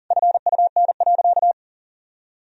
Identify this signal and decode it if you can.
Morse code, heard as a keyed tone, about 40 wpm, 700 Hz. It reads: FUN.
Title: Sentences from Top 1000 Words - Review Part 4 - 40wpm